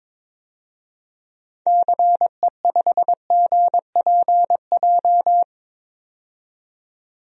Morse code, heard as a keyed tone, about 22 wpm, 700 Hz. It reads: CE5GPJ
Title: DXCC-100 Call Signs - 6 Characters - 22wpm